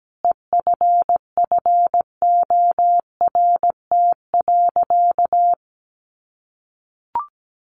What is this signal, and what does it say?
Morse code, heard as a keyed tone, about 17 wpm, 700 Hz. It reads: EFFORT.
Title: Sentences from Top 1200 Words 17wpm